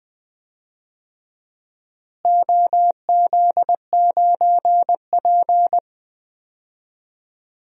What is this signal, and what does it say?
Morse code, heard as a keyed tone, about 20 wpm, 700 Hz. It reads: OZ9P
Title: DXCC-100 Call Signs - 4 Characters - Repeated 3x - 20wpm